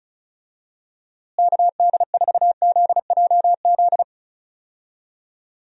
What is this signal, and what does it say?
Morse code, heard as a keyed tone, about 34 wpm, 700 Hz. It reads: KD4ZJZ
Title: USA Call Signs - 6 Characters - Repeated 2x - 35wpm